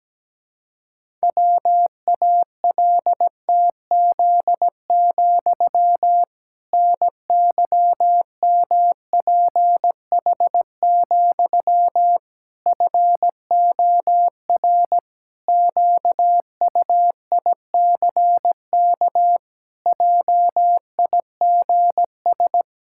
Morse code, heard as a keyed tone, about 17 wpm, 700 Hz. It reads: WALTZ, NYMPH, FOR QUICK JIGS
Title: Pangram Challenge v2 - 17wpm